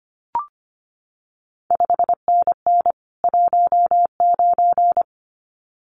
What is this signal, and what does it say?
Morse code, heard as a keyed tone, about 25 wpm, 700 Hz. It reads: 5NN 19
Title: CQ WW DX Contest Exchange Encore - 25wpm